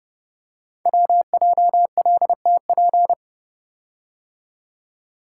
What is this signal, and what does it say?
Morse code, heard as a keyed tone, about 30 wpm, 700 Hz. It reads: WJLTP
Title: Code Groups - 5 Letters - 30wpm